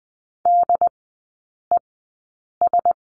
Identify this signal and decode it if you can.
Morse code, heard as a keyed tone, about 20 wpm, spaced as for 8 wpm, 700 Hz. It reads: DES
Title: 081 Lesson 14e - Number 2 - Three Characters 20wpm